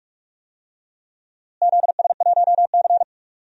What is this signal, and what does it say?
Morse code, heard as a keyed tone, about 42 wpm, 700 Hz. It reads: ZS1C
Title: DXCC-100 Call Signs - 4 Characters - Repeated 3x - 45wpm